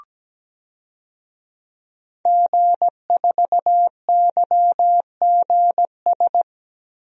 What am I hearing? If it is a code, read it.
Morse code, heard as a keyed tone, about 17 wpm, 700 Hz. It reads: G4YGS
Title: DXCC-100 Call Signs - 5 Characters - 17wpm